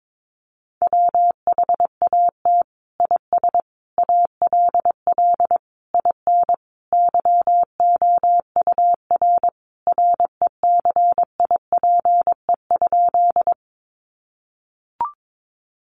Morse code, heard as a keyed tone, about 22 wpm, 700 Hz. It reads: WHAT IS ALL IN YOUR RECIPE?
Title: Sentences from Top 1200 Words 22wpm